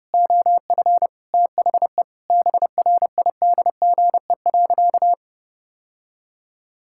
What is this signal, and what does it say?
Morse code, heard as a keyed tone, about 30 wpm, 700 Hz. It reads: OF THE BRIDGE.
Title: Sentences from Top 1600 Words 30wpm